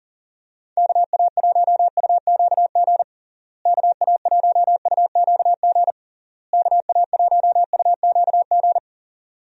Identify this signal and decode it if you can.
Morse code, heard as a keyed tone, about 40 wpm, 700 Hz. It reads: KA1UQG KA1UQG KA1UQG
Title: USA Call Signs - 6 Characters - Repeated 3x - 40wpm